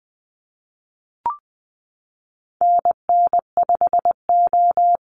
Morse code, heard as a keyed tone, about 20 wpm, 700 Hz. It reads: NN5O
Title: Call Signs Repeated 2x - 20wpm